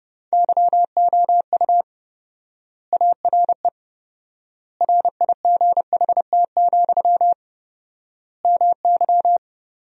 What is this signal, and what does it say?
Morse code, heard as a keyed tone, about 30 wpm, 700 Hz. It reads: YOU ARE RIGHT, MY
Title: 4x Word-Spacing - Sentences from Top 400 Words - 30wpm